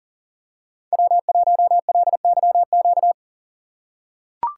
Morse code, heard as a keyed tone, about 40 wpm, 700 Hz. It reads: W1LYQ
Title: USA Call Signs - 3x Character Spacing - 40wpm